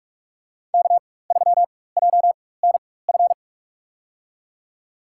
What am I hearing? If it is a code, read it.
Morse code, heard as a keyed tone, about 44 wpm, spaced as for 21 wpm, 700 Hz. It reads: K3JNF
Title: USA Call Signs - 4x Character Spacing - 45wpm